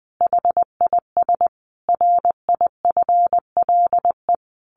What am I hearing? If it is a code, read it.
Morse code, heard as a keyed tone, about 20 wpm, 700 Hz. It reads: HIS RIFLE
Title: A Sound Of Thunder 20wpm